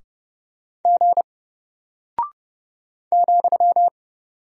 Morse code, heard as a keyed tone, about 30 wpm, 700 Hz. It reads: G ,
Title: Single Character (Letter:Number:Punctuation:Prosign) Alt 30wpm